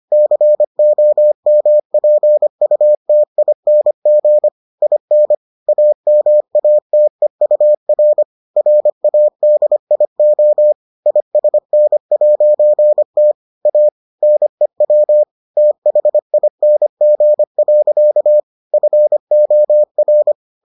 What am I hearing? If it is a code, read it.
Morse code, heard as a keyed tone, about 25 wpm, 600 Hz. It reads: COMPUTING IN AMATEUR RADIO ISN'T A NEW THING. FOR